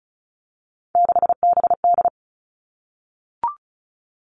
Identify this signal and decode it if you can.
Morse code, heard as a keyed tone, about 36 wpm, 700 Hz. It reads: 6BD
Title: Three Challenging Letter-Number Combinations 35wpm